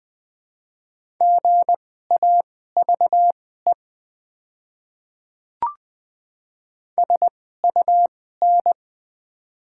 Morse code, heard as a keyed tone, about 20 wpm, spaced as for 14 wpm, 700 Hz. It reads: GAVE SUN
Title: 2x Character Spacing - Top 500 Words - 20wpm